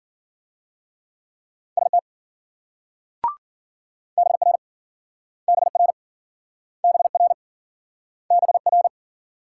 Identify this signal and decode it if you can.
Morse code, heard as a keyed tone, about 53 wpm, 700 Hz. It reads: ST BR BR BR BR